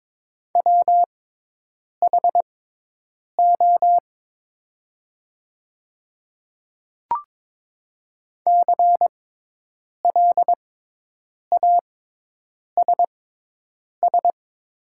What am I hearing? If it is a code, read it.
Morse code, heard as a keyed tone, about 22 wpm, spaced as for 8 wpm, 700 Hz. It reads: WHO CLASS
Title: 6x Character Spacing - Top 500 Words - 22wpm